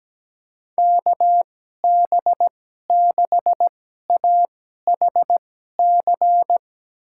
Morse code, heard as a keyed tone, about 17 wpm, spaced as for 12 wpm, 700 Hz. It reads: KB6AHC
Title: USA Call Signs - 6 Characters - 2x Character Spacing - 17wpm